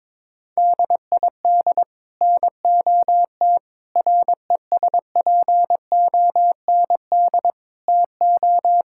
Morse code, heard as a keyed tone, about 22 wpm, 700 Hz. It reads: DID NOT RESPOND TO
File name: practice / 22wpm / Sets of 8 Words - 22wpm